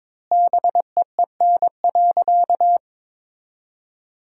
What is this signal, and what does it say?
Morse code, heard as a keyed tone, about 22 wpm, 700 Hz. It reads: BEEN.
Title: 2x Word-Spacing - Sentences from Top 300 Words - 22wpm